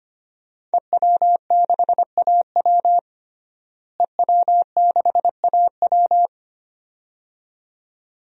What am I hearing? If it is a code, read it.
Morse code, heard as a keyed tone, about 25 wpm, 700 Hz. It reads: EW6AW EW6AW
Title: DXCC-100 Call Signs - 5 Characters - Repeated 2x - 25wpm